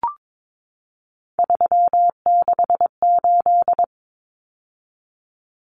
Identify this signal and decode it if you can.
Morse code, heard as a keyed tone, about 22 wpm, 700 Hz. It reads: 368
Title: Numbers - 3-Digit 22wpm